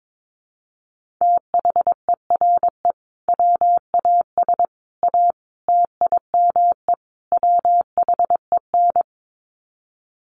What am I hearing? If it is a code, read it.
Morse code, heard as a keyed tone, about 22 wpm, 700 Hz. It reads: THERE WAS A TIME WHEN
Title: Sets of 5 Words 22wpm